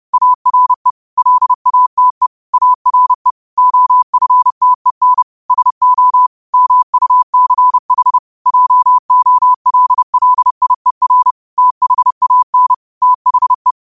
Morse code, heard as a keyed tone, about 30 wpm, 1000 Hz. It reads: ARE LATE ARE OFTEN SO MUCH JOLLIER THAN THE
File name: qotdfiles / 30wpm